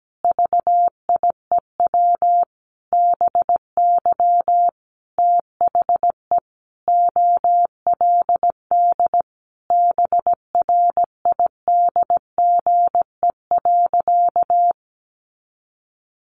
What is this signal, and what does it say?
Morse code, heard as a keyed tone, about 17 wpm, 700 Hz. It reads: VIEW BY THE OLD BRIDGE.
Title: Sentences from Top 1200 Words 17wpm